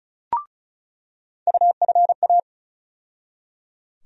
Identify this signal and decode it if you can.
Morse code, heard as a keyed tone, about 34 wpm, 700 Hz. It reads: UFA
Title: USA Call Sign Postfixes - 3 Letters 35wpm